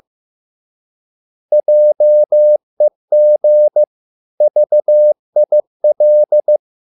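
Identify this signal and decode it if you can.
Morse code, heard as a keyed tone, about 15 wpm, 600 Hz. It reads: JEG VIL
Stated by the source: Morse code at a slow 15 wpm from CW med Gnister.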